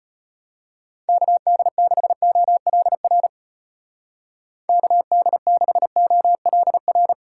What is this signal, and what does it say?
Morse code, heard as a keyed tone, about 31 wpm, 700 Hz. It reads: KD6OLR KD6OLR
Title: Speed Racing - USA 2x3 Call Signs 25wpm